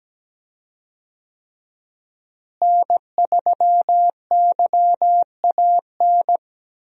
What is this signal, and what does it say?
Morse code, heard as a keyed tone, about 17 wpm, 700 Hz. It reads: N3YAN
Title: USA Call Signs - 5 Characters - Repeated 3x - 17wpm